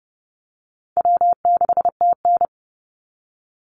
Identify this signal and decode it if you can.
Morse code, heard as a keyed tone, about 30 wpm, 700 Hz. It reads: W6TN
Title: Easy Call Signs - 30wpm